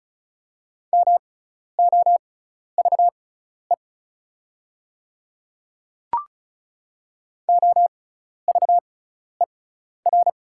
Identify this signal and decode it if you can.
Morse code, heard as a keyed tone, about 36 wpm, spaced as for 12 wpm, 700 Hz. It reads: MOVE OVER
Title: 6x Character Spacing - Top 300 Words - 35wpm